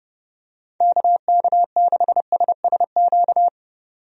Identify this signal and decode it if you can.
Morse code, heard as a keyed tone, about 30 wpm, 700 Hz. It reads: KK6SSQ
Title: USA Call Signs - 6 Characters - 30wpm